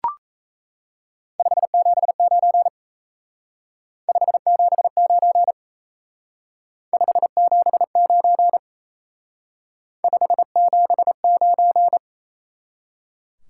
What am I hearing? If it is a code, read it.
Morse code, heard as a keyed tone, about 36 wpm, 700 Hz. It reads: 579 579 579 579